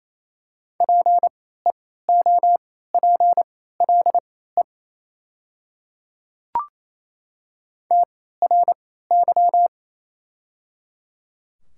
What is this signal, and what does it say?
Morse code, heard as a keyed tone, about 27 wpm, spaced as for 16 wpm, 690 Hz. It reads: PEOPLE TRY
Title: 3x Character Spacing - Top 200 Words - 28wpm